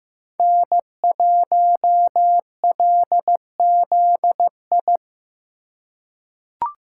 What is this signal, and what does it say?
Morse code, heard as a keyed tone, about 16 wpm, 700 Hz. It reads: N1LZI
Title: USA Call Signs - 8x Character Spacing - 15wpm